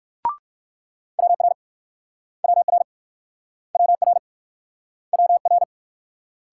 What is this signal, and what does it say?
Morse code, heard as a keyed tone, about 61 wpm, 700 Hz. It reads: WR WR WR WR